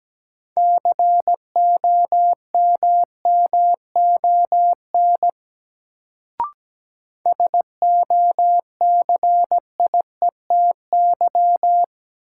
Morse code, heard as a keyed tone, about 17 wpm, 700 Hz. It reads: COMMON SOCIETY
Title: Sentences from Top 700 Words - New Word Drill - 17wpm